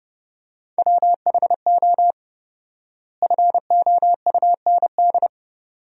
Morse code, heard as a keyed tone, about 32 wpm, 700 Hz. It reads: WHO FOUND
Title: 4x Word-Spacing - Sentences from Top 300 Words - 30wpm